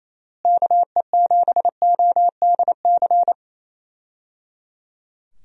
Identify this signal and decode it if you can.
Morse code, heard as a keyed tone, about 28 wpm, 700 Hz. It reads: KE7ODC
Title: USA Call Signs - 6 Characters - Repeated 3x - 28wpm